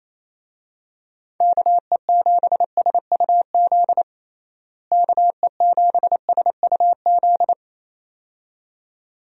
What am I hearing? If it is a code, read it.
Morse code, heard as a keyed tone, about 27 wpm, 700 Hz. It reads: KE7SUZ KE7SUZ
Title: USA Call Signs - 6 Characters - Repeated 2x - 28wpm